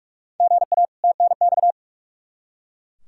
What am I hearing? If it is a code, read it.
Morse code, heard as a keyed tone, about 44 wpm, 700 Hz. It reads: GA TNX
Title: Practice QSO - Exchange 1 - Beginning - 45wpm